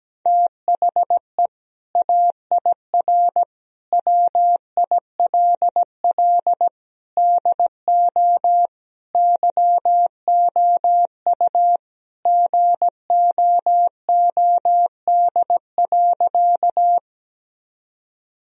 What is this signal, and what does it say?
Morse code, heard as a keyed tone, about 17 wpm, 700 Hz. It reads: THE AIR WILL DO YOU GOOD.
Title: Sentences from Top 1000 Words - Review Part 1 - 17wpm